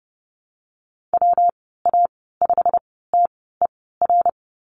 Morse code, heard as a keyed tone, about 30 wpm, spaced as for 17 wpm, 700 Hz. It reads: WA5TER
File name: Lesson 19g - Letter w - Call Signs 30wpm